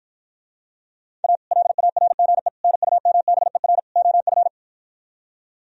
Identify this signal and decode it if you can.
Morse code, heard as a keyed tone, about 53 wpm, 700 Hz. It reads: A LARGE NUMBER OF